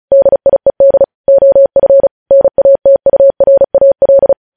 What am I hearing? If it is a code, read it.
Morse code, heard as a keyed tone, about 36 wpm, 550 Hz. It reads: DIED OF NATURAL
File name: podcast / News headlines-1x-35x35